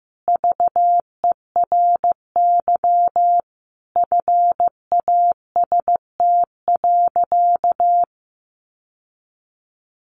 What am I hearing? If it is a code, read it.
Morse code, heard as a keyed tone, about 14 wpm, 700 Hz. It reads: VERY FAST.
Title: Sentences from Top 1100 Words 15wpm